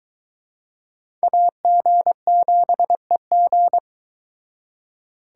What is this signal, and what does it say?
Morse code, heard as a keyed tone, about 24 wpm, 700 Hz. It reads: AG7EG